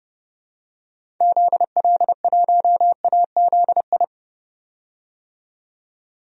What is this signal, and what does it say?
Morse code, heard as a keyed tone, about 30 wpm, 700 Hz. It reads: ZL1AZI
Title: DXCC-100 Call Signs - 6 Characters - 30wpm